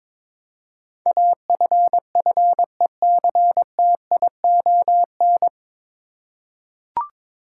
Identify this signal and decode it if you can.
Morse code, heard as a keyed tone, about 22 wpm, 700 Hz. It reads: AFFECTION
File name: 9 Letter Words - First 3 Letters Given - 22wpm